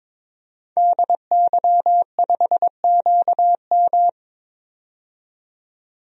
Morse code, heard as a keyed tone, about 22 wpm, 700 Hz. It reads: DY5QM
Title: DXCC-100 Call Signs - 5 Characters - 22wpm